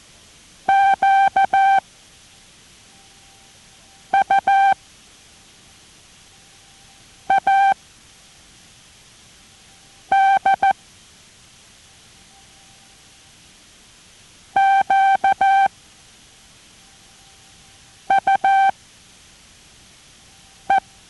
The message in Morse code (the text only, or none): Q U A D Q U E